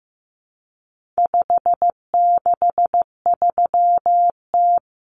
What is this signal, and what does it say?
Morse code, heard as a keyed tone, about 15 wpm, 700 Hz. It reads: 563T
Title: Common Cut Numbers - 4-Digit 15wpm